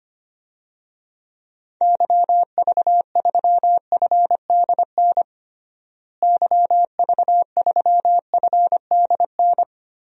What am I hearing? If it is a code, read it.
Morse code, heard as a keyed tone, about 25 wpm, 700 Hz. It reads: YV3FDN YV3FDN
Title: DXCC-100 Call Signs - 6 Characters - Repeated 3x - 25wpm